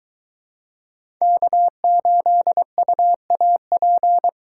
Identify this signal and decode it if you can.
Morse code, heard as a keyed tone, about 23 wpm, 700 Hz. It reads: K8UAP